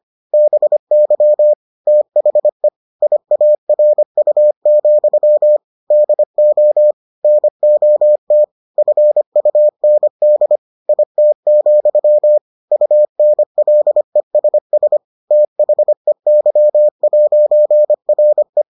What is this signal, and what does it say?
Morse code, heard as a keyed tone, about 25 wpm, 600 Hz. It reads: BY THE IARU, DO NOT FUND IT, UNLESS THEY'RE